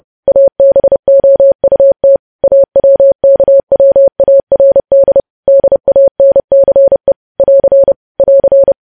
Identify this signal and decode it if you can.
Morse code, heard as a keyed tone, about 30 wpm, 550 Hz. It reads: ABOUT AWKWARD DANCE <AR> <AR>